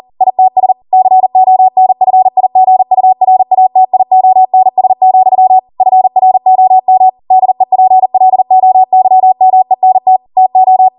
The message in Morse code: ITS CONFIGURATIONS, FROM DEPLOYMENT TO

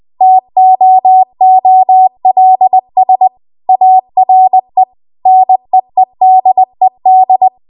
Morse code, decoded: TOOLS ARE NEEDED

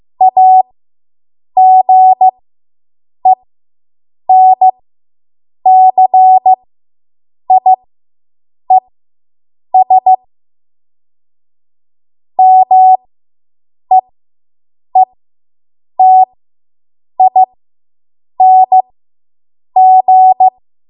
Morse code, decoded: AGENCIES MEETING

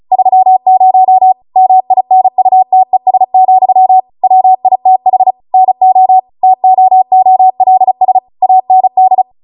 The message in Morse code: 30 MINUTES, WITH NO TOOLS AND

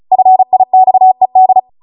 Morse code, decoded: FIXED

